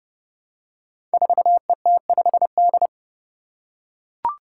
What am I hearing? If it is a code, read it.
Morse code, heard as a keyed tone, about 30 wpm, 700 Hz. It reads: 4ET5D
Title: Five Challenging Letter-Number Combinations 30wpm